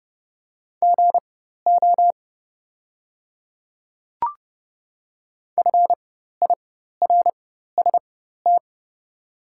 Morse code, heard as a keyed tone, about 30 wpm, spaced as for 14 wpm, 700 Hz. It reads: GO FIRST
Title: 4x Character Spacing - Top 100 Words - 30wpm